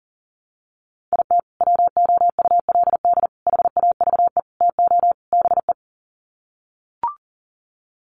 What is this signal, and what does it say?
Morse code, heard as a keyed tone, about 40 wpm, 700 Hz. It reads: IT WOULD HAVE TO BE
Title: Sets of 5 Words Encore 40wpm